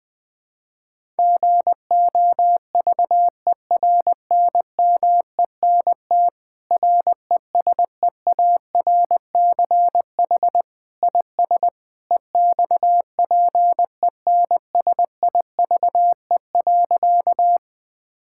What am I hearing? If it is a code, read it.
Morse code, heard as a keyed tone, about 20 wpm, 700 Hz. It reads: GOVERNMENT RESEARCH IS EXPENSIVE.
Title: Sentences from Top 1000 Words - Review Part 6 - 20wpm